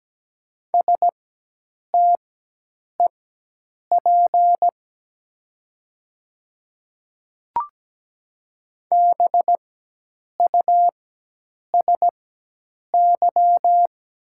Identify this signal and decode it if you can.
Morse code, heard as a keyed tone, about 17 wpm, spaced as for 8 wpm, 690 Hz. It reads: STEP BUSY
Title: 4x Character Spacing - Top 500 Words - 17wpm